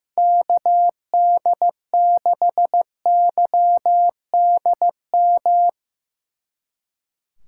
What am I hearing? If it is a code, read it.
Morse code, heard as a keyed tone, about 17 wpm, 700 Hz. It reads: KD6YDM
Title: Speed Racing - USA Call Signs 15wpm